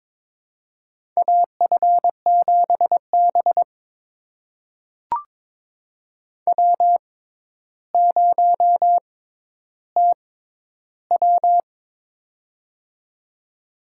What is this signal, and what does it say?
Morse code, heard as a keyed tone, about 22 wpm, spaced as for 11 wpm, 700 Hz. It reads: AF7B W0TW
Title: USA Call Signs - 6x Character Spacing - 22wpm